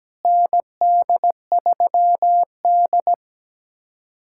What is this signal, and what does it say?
Morse code, heard as a keyed tone, about 17 wpm, 700 Hz. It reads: ND3D
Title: K1USN SST QSO Practice - 17wpm